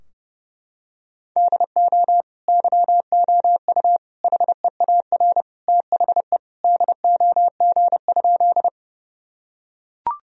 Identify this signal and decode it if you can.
Morse code, heard as a keyed tone, about 30 wpm, 700 Hz. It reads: DO YOU HEAR THE DOG?
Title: Sentences from Top 1000 Words - Review Part 2 - 30wpm